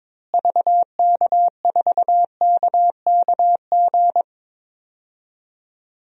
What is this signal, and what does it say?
Morse code, heard as a keyed tone, about 22 wpm, 700 Hz. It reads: VK4KKG
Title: DXCC-100 Call Signs - 6 Characters - Repeated 2x - 22wpm